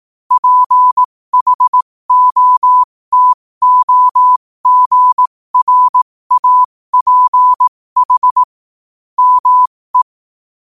Morse code, heard as a keyed tone, about 18 wpm, spaced as for 15 wpm, 1000 Hz. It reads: PHOTOGRAPH ME
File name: qotdfiles / 15wpm